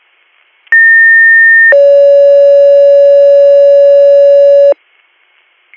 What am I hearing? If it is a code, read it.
Two-tone page, heard as a one-second tone, then a three-second tone, about 1800 then 560 Hz. It is heard from 0.7 to 4.7 s.